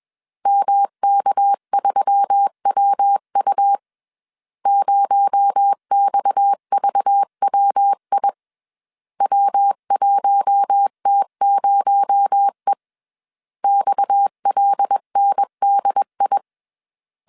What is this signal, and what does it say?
Morse code, heard as a keyed tone, about 21 wpm, 800 Hz. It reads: MK3WU 0XVWI W1T0E XLNDI
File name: CW / DET / 1 6 1500C S A 20wpm 2048